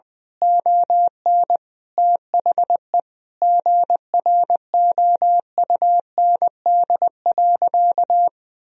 Morse code, heard as a keyed tone, about 20 wpm, 700 Hz. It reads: ON THE GROUND.
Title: Sentences from Top 1000 Words - Review Part 2 - 20wpm